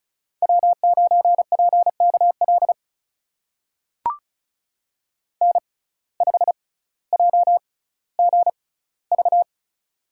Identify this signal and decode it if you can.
Morse code, heard as a keyed tone, about 36 wpm, spaced as for 18 wpm, 700 Hz. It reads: W9PKL N5JGV
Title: USA Call Signs - 6x Character Spacing - 35wpm